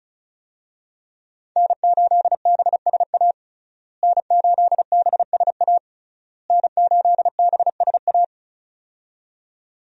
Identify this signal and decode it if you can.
Morse code, heard as a keyed tone, about 36 wpm, 700 Hz. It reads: N8BSA N8BSA N8BSA
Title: USA Call Signs - 5 Characters - Repeated 3x - 35wpm